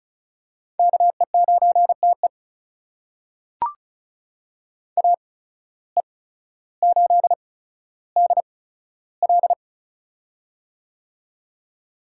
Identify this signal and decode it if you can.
Morse code, heard as a keyed tone, about 36 wpm, spaced as for 15 wpm, 700 Hz. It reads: KE9TE AE8DL
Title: USA Call Signs - 5 Characters - 8x Character Spacing - 35wpm